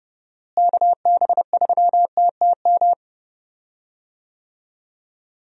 Morse code, heard as a keyed tone, about 30 wpm, 700 Hz. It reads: KB3TTM
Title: USA Call Signs - 6 Characters - Repeated 3x - 30wpm